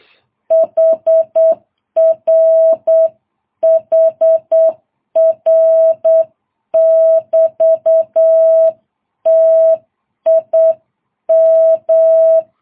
Morse code, heard as a keyed tone, about 8 wpm, 650 Hz. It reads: HR HR <BT> TIM
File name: jh8cbh